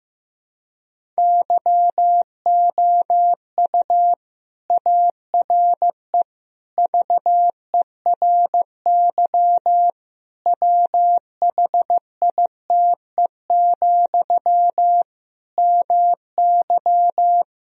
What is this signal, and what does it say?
Morse code, heard as a keyed tone, about 15 wpm, 700 Hz. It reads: YOU ARE VERY WHITE, MY
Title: Sentences from Top 1000 Words - Review Part 1 - 15wpm